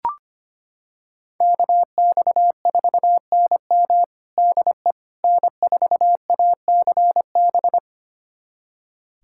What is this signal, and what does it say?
Morse code, heard as a keyed tone, about 25 wpm, 700 Hz. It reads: KX4NM DE N4ACB
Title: USA Callsign DE Callsign 25wpm